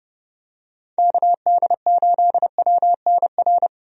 Morse code, heard as a keyed tone, about 30 wpm, 700 Hz. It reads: KD8WNR
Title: USA Call Signs - 6 Characters - 30wpm